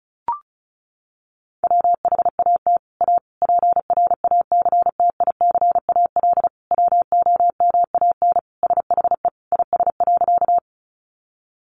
Morse code, heard as a keyed tone, about 34 wpm, 700 Hz. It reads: WHAT A PRACTICAL WOMAN SHE IS.
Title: Sentences from Top 1400 Words 35wpm